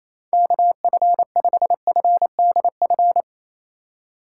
Morse code, heard as a keyed tone, about 28 wpm, 700 Hz. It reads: KF5FDF
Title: USA Call Signs - 2x Character Spacing - 28wpm